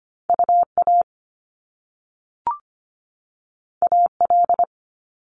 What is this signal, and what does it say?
Morse code, heard as a keyed tone, about 25 wpm, 700 Hz. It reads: UA AL
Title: Top Two Letter Combinations in Words 25wpm